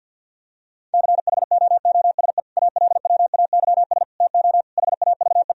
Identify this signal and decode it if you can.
Morse code, heard as a keyed tone, about 50 wpm, 700 Hz. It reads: CHOOSE ALWAYS TO HAVE
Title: Sentences from Top 1000 Words - Review Part 4 - 50wpm